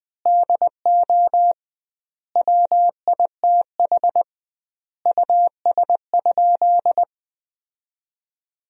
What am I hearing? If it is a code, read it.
Morse code, heard as a keyed tone, about 21 wpm, 700 Hz. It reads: DO WITH US?
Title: 2x Word-Spacing - Sentences from Top 200 Words - 20wpm